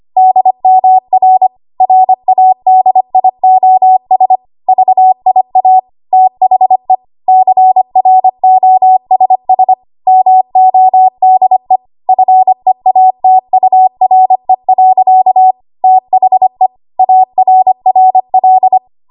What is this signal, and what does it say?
Morse code, heard as a keyed tone, about 25 wpm, 750 Hz. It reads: DMR RADIOS VIA THE CROSS MODE FEATURE. THE ARRL
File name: Archive / 25 WPM / 25WPM